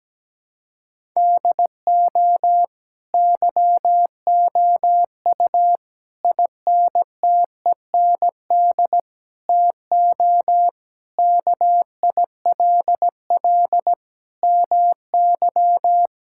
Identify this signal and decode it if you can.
Morse code, heard as a keyed tone, about 17 wpm, 700 Hz. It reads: DO YOU INTEND TO KILL MY
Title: Sentences from Top 1600 Words 17wpm